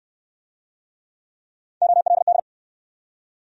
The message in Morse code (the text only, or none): OLD